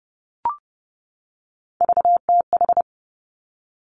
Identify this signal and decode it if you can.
Morse code, heard as a keyed tone, about 30 wpm, 700 Hz. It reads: VTH
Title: Three Challenging Letter-Number Combinations 30wpm